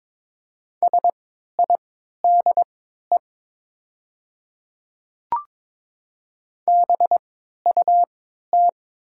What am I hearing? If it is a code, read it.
Morse code, heard as a keyed tone, about 22 wpm, spaced as for 12 wpm, 700 Hz. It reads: SIDE BUT
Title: 3x Character Spacing - Top 100 Words - 22wpm